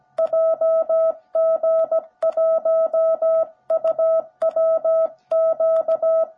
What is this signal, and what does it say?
Morse code, heard as a keyed tone, about 17 wpm, 650 Hz. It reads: JG1UWQ
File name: jh8cbh / 2021cw / cw17b